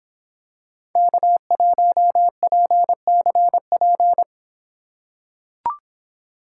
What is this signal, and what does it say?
Morse code, heard as a keyed tone, about 26 wpm, 700 Hz. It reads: K1PCP